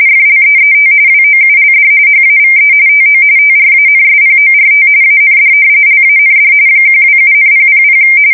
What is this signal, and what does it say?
RTTY, about 45.45 baud. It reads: FLEET WOULD HAVE TO TRANSPORT AND LAND 127,000
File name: fsk170 / battleshipnj